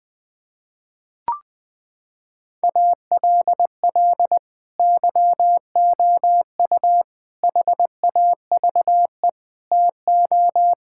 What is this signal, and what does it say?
Morse code, heard as a keyed tone, about 20 wpm, 700 Hz. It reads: ALL YOU HAVE TO
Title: Sets of 8 Words - 20wpm